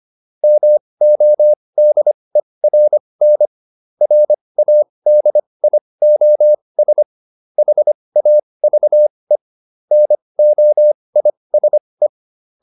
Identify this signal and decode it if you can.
Morse code, heard as a keyed tone, about 25 wpm, spaced as for 20 wpm, 600 Hz. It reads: MODERN RADIOS HAVE NOISE